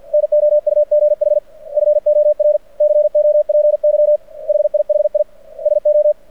Morse code, heard as a keyed tone, about 49 wpm, 600 Hz. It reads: NOUGH FOR COPY HIHI SO
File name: archived / CRSnet